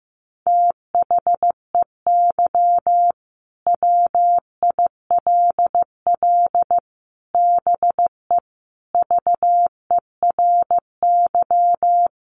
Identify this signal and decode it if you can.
Morse code, heard as a keyed tone, about 15 wpm, 700 Hz. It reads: THEY WILL BE VERY
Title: Sentences from Top 1100 Words 15wpm